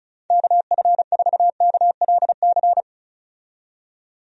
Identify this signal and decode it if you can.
Morse code, heard as a keyed tone, about 35 wpm, 700 Hz. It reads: KF4KLC
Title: USA Call Signs - 6 Characters - 35wpm